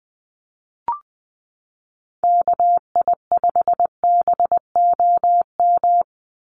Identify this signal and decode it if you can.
Morse code, heard as a keyed tone, about 20 wpm, 700 Hz. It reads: KI5BOM
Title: USA Call Signs 20wpm